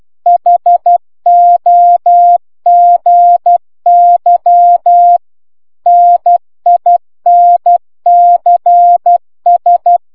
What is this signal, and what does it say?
Morse code, heard as a keyed tone, about 12 wpm, 700 Hz. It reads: HOGY NINCS